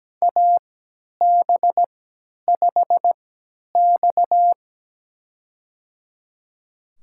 Morse code, heard as a keyed tone, about 17 wpm, spaced as for 10 wpm, 700 Hz. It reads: AB5X
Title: USA Call Signs - 4 Characters - 3x Character Spacing - 17wpm